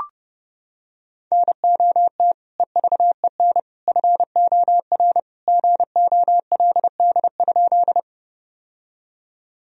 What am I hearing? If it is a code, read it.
Morse code, heard as a keyed tone, about 30 wpm, 700 Hz. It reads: NOT EVEN FOR GOLD?
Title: Sentences from Top 1000 Words - Review Part 3 - 30wpm